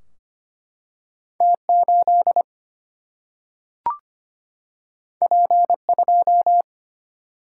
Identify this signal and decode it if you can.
Morse code, heard as a keyed tone, about 25 wpm, 700 Hz. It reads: T8 P2
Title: Single-Letter Single-Number 25wpm